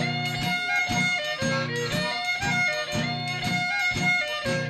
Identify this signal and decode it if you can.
Emergency Alert System (SAME) tones, heard as harsh buzzing bursts. It is heard from 0.5 to 4.3 s.